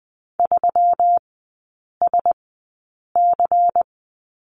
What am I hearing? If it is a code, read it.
Morse code, heard as a keyed tone, about 20 wpm, spaced as for 8 wpm, 700 Hz. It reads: 3SC
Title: 133 Lesson 22e - Question Mark - Three Characters 20wpm